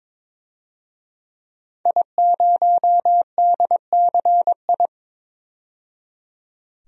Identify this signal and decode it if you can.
Morse code, heard as a keyed tone, about 22 wpm, 700 Hz. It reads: I0DCI
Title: DXCC-100 Call Signs - 5 Characters - 22wpm